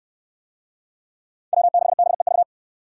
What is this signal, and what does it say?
Morse code, heard as a keyed tone, about 62 wpm, 700 Hz. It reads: W6BEV